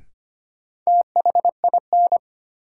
Morse code, heard as a keyed tone, about 25 wpm, 700 Hz. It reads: THIN
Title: Sentences from Top 1200 Words - New Word Drill - 25wpm